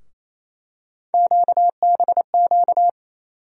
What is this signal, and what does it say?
Morse code, heard as a keyed tone, about 28 wpm, 700 Hz. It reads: QBQ
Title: USA Call Sign Postfixes - 3 Letters 28wpm